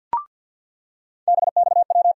Morse code, heard as a keyed tone, about 54 wpm, 700 Hz. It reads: BXW
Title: USA Call Sign Postfixes - 3 Letters 50wpm